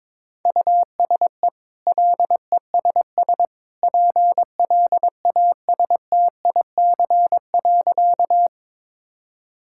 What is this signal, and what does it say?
Morse code, heard as a keyed tone, about 21 wpm, 690 Hz. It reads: USE LESS PLASTIC.
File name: practice / 22wpm / Easy Sentences with Hints - 22wpm